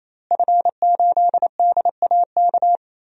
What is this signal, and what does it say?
Morse code, heard as a keyed tone, about 28 wpm, 700 Hz. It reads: F8DAK
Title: DXCC-100 Call Signs - 5 Characters - Repeated 3x - 28wpm